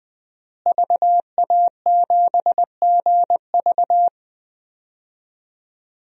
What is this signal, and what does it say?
Morse code, heard as a keyed tone, about 20 wpm, 700 Hz. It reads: VA7GV
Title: DXCC-100 Call Signs - 5 Characters - Repeated 2x - 20wpm